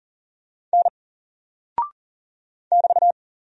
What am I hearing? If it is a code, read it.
Morse code, heard as a keyed tone, about 40 wpm, 700 Hz. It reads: N <BT>
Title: Single Character (Letter:Number:Punctuation:Prosign) 40wpm